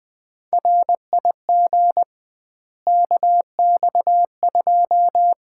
Morse code, heard as a keyed tone, about 21 wpm, 700 Hz. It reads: RIG KX2
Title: Common QSO Phrases - 2x Word-Spacing - 20wpm